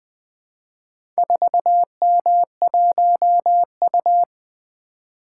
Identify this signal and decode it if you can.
Morse code, heard as a keyed tone, about 20 wpm, 700 Hz. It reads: 4M1U
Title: Four Letter-Number Combinations - 20wpm